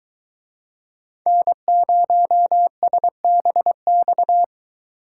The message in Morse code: N0SBX

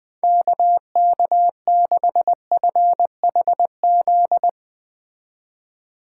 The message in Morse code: KK6FHZ

